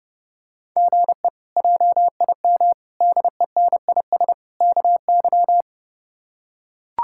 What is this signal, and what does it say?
Morse code, heard as a keyed tone, about 32 wpm, 700 Hz. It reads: GE JIM DENIS KY
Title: K1USN SST QSO Practice - 30wpm